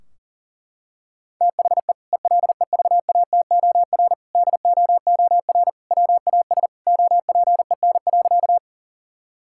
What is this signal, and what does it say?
Morse code, heard as a keyed tone, about 40 wpm, 700 Hz. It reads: THE ELEVATOR DOOR WAS OPEN.
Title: Sentences from Top 1600 Words 40wpm